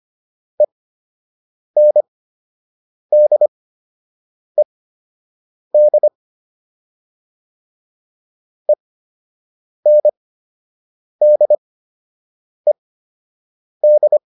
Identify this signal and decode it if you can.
Morse code, heard as a keyed tone, about 25 wpm, spaced as for 7 wpm, 600 Hz. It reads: ENDED ENDED